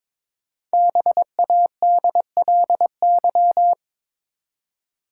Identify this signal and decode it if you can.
Morse code, heard as a keyed tone, about 22 wpm, 700 Hz. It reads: BADLY
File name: 5 Letter Words - First 3 Letters Given - 22wpm